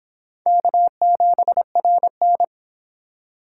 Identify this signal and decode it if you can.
Morse code, heard as a keyed tone, about 30 wpm, 700 Hz. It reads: K7RN